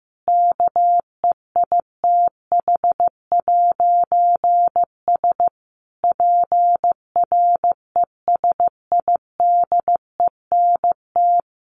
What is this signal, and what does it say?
Morse code, heard as a keyed tone, about 15 wpm, 700 Hz. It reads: KEITH'S PRESIDENT
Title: A Sound Of Thunder 15wpm